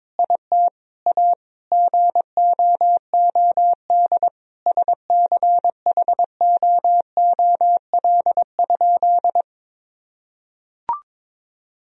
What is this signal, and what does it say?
Morse code, heard as a keyed tone, about 22 wpm, 700 Hz. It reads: IT A GOOD SCHOOL?
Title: Sentences from Top 1000 Words - Review Part 1 - 22wpm